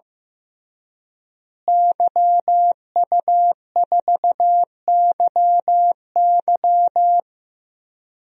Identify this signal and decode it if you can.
Morse code, heard as a keyed tone, about 15 wpm, 700 Hz. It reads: YU4YY
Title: DXCC-100 Call Signs - 5 Characters - Repeated 3x - 15wpm